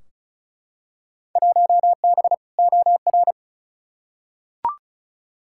Morse code, heard as a keyed tone, about 36 wpm, 700 Hz. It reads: 1B OR
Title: Field Day Exchanges - 35wpm